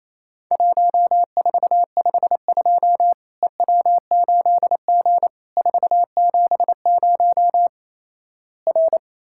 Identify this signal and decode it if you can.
Morse code, heard as a keyed tone, about 28 wpm, 700 Hz. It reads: 1452 EW8G 470 R
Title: Worked All Europe DX Contest - QTC Exchange - 28wpm